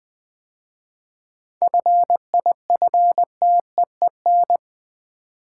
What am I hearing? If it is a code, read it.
Morse code, heard as a keyed tone, about 20 wpm, 700 Hz. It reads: FIFTEEN
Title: 7 Letter Words - First 3 Letters Given - 20wpm